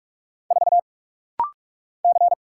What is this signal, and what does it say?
Morse code, heard as a keyed tone, about 46 wpm, 700 Hz. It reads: 4 C
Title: Single Letter-Number Alt - Rapid-Fire 45wpm